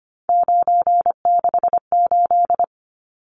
Morse code, heard as a keyed tone, about 25 wpm, 700 Hz. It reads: 968
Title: Numbers - 3-Digit 25wpm